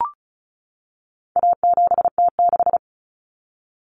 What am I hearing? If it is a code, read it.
Morse code, heard as a keyed tone, about 34 wpm, 700 Hz. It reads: A7T6
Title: Common Cut Numbers - 4-Digit 35wpm